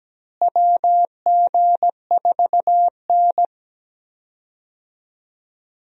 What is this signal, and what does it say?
Morse code, heard as a keyed tone, about 17 wpm, 700 Hz. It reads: WG4N
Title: USA Call Signs - 4 Characters - Repeated 3x - 17wpm